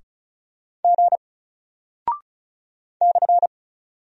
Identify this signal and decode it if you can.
Morse code, heard as a keyed tone, about 36 wpm, 700 Hz. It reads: G /
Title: Single Character (Letter:Number:Punctuation:Prosign) 35wpm